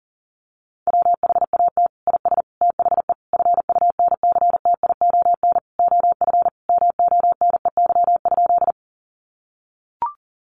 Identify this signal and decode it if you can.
Morse code, heard as a keyed tone, about 40 wpm, 700 Hz. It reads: WHAT IS THE FUNCTION OF MONEY?
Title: Sentences from Top 1100 Words 40wpm